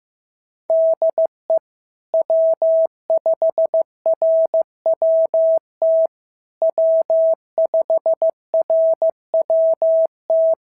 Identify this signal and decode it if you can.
Morse code, heard as a keyed tone, about 14 wpm, 650 Hz. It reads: DE W5RWT W5RWT
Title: SOTA QSO Practice 15wpm